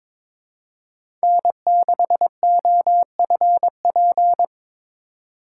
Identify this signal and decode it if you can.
Morse code, heard as a keyed tone, about 22 wpm, 700 Hz. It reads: N6OFP
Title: USA Call Signs - 5 Characters - 4x Character Spacing - 22wpm